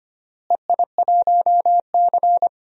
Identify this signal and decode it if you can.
Morse code, heard as a keyed tone, about 25 wpm, 700 Hz. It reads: EI1C
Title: DXCC-100 Call Signs - 4 Characters - Repeated 3x - 25wpm